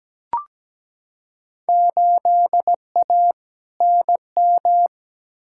Morse code, heard as a keyed tone, about 17 wpm, 700 Hz. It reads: 8A NM
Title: Field Day Exchanges Encore - 17wpm